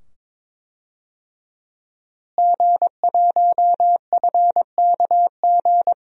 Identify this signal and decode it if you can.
Morse code, heard as a keyed tone, about 22 wpm, 700 Hz. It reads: G1FKG
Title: DXCC-100 Call Signs - 5 Characters - Repeated 2x - 22wpm